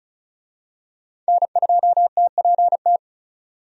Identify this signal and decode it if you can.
Morse code, heard as a keyed tone, about 34 wpm, 700 Hz. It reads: N2TPT
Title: USA Call Signs - 5 Characters - 35wpm